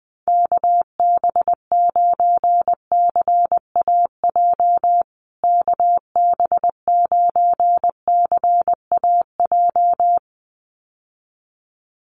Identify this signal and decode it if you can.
Morse code, heard as a keyed tone, about 20 wpm, 700 Hz. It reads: KB9CAJ KB9CAJ
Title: USA Call Signs Repeated 2x 20wpm